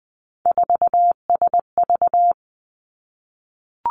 Morse code, heard as a keyed tone, about 20 wpm, 700 Hz. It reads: 4SV
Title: Three Challenging Letter-Number Combinations 20wpm